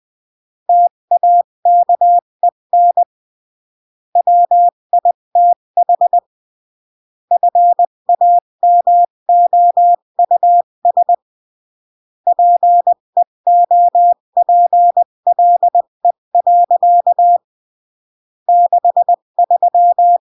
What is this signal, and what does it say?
Morse code, heard as a keyed tone, about 20 wpm, 700 Hz. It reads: TAKEN WITH FAMOUS PEOPLE. 63
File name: CW / jodorowsky / 82 commandments 20-15